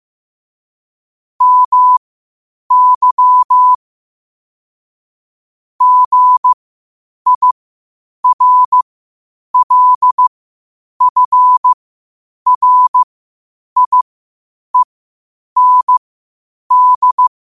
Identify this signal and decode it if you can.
Morse code, heard as a keyed tone, about 15 wpm, spaced as for 8 wpm, 1000 Hz. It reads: MY GIRLFRIEND
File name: qotdfiles / 8wpm